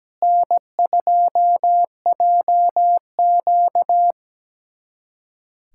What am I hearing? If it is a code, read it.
Morse code, heard as a keyed tone, about 17 wpm, 700 Hz. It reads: N2JQ
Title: USA Callsign DE Callsign - 4 Characters - 17wpm